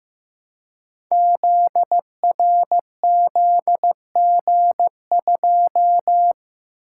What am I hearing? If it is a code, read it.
Morse code, heard as a keyed tone, about 15 wpm, 700 Hz. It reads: ZRZG2
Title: Code Groups - 5 Letter:Numbers - 15wpm